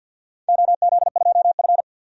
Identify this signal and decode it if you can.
Morse code, heard as a keyed tone, about 53 wpm, 700 Hz. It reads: OZ2F
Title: DXCC-100 Call Signs - 4 Characters - 50wpm